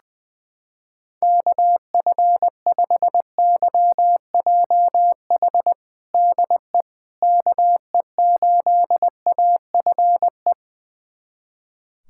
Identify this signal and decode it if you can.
Morse code, heard as a keyed tone, about 20 wpm, 700 Hz. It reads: KF5YJH DE KE8AFE
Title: USA Callsign DE Callsign - 6 Characters - 20wpm